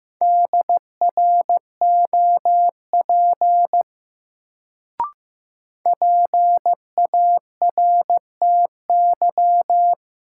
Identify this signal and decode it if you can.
Morse code, heard as a keyed tone, about 15 wpm, 700 Hz. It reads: DROP PARTY
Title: Sentences from Top 1400 Words - New Word Drill Alt - 15wpm